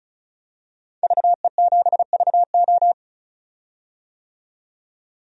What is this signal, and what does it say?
Morse code, heard as a keyed tone, about 36 wpm, 700 Hz. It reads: VE7VO
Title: DXCC-100 Call Signs - 5 Characters - 35wpm